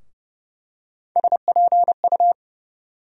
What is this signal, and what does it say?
Morse code, heard as a keyed tone, about 30 wpm, 700 Hz. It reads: SPU
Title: USA Call Sign Postfixes - 3 Letters 30wpm